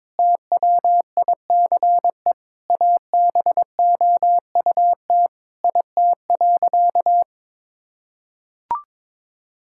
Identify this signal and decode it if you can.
Morse code, heard as a keyed tone, about 22 wpm, 700 Hz. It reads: TWICE ABOUT IT.
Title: Sentences from Top 1600 Words 22wpm